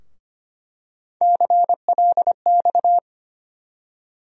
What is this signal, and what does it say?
Morse code, heard as a keyed tone, about 25 wpm, 700 Hz. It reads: CLX
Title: USA Call Sign Postfixes - 3 Letters 25wpm